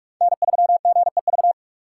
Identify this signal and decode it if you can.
Morse code, heard as a keyed tone, about 46 wpm, 700 Hz. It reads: N3GEV
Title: Code Groups - 5 Letter:Numbers - 45wpm